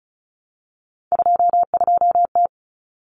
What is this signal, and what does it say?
Morse code, heard as a keyed tone, about 34 wpm, 700 Hz. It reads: 22T
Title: Common Cut Numbers - 3-Digit 35wpm